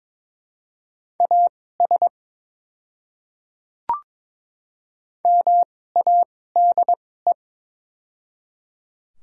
Morse code, heard as a keyed tone, about 22 wpm, spaced as for 16 wpm, 700 Hz. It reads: AS MADE
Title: 2x Character Spacing - Top 200 Words - 22wpm